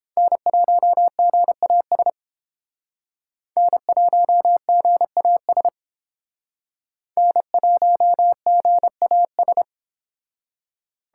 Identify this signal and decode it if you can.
Morse code, heard as a keyed tone, about 29 wpm, 700 Hz. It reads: N1GAS N1GAS N1GAS